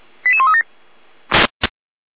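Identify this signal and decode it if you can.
Five-tone selective call (ZVEI), heard as quick runs of five tones, tones of about 70 ms. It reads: 88137
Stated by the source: Received on an SDR, ZVEI five-tone selective call, code 88139.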